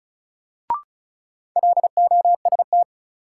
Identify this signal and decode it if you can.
Morse code, heard as a keyed tone, about 34 wpm, 700 Hz. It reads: LOST
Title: Sentences from Top 1400 Words - New Word Drill - 35wpm